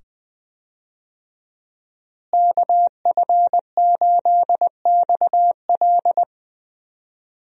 Morse code, heard as a keyed tone, about 20 wpm, 700 Hz. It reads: KF8XL
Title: USA Call Signs - 5 Characters - Repeated 2x - 20wpm